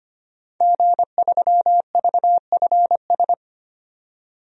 Morse code, heard as a keyed tone, about 25 wpm, 700 Hz. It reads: G3VFS
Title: DXCC-100 Call Signs - 5 Characters - 25wpm